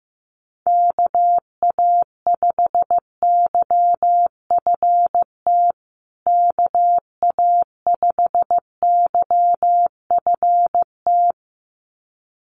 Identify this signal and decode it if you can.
Morse code, heard as a keyed tone, about 15 wpm, 700 Hz. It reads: KA5YFT KA5YFT
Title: USA Call Signs Repeated 2x 15wpm